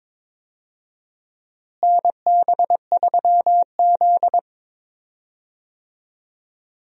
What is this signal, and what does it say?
Morse code, heard as a keyed tone, about 22 wpm, 700 Hz. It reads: NB3Z
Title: USA Call Signs - 4 Characters - Repeated 3x - 22wpm